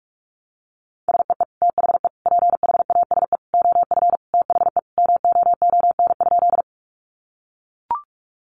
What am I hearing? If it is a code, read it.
Morse code, heard as a keyed tone, about 44 wpm, 700 Hz. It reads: SEE THE PHASE OF THE MOON?
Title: Sentences from Top 1200 Words 45wpm